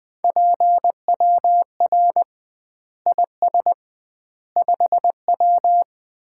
Morse code, heard as a keyed tone, about 20 wpm, 700 Hz. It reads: PWR IS 5W
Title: Common QSO Phrases - 2x Word-Spacing - 20wpm